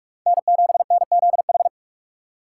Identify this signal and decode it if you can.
Morse code, heard as a keyed tone, about 46 wpm, 700 Hz. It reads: N7NZH